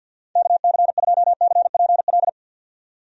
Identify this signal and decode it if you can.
Morse code, heard as a keyed tone, about 46 wpm, 700 Hz. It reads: KC2CPL
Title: USA Call Signs - 6 Characters - 50wpm